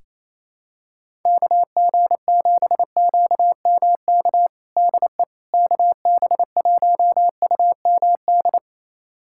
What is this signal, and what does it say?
Morse code, heard as a keyed tone, about 28 wpm, 700 Hz. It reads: KG7QMK DE KB1UMD
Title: USA Callsign DE Callsign - 6 Characters - 28wpm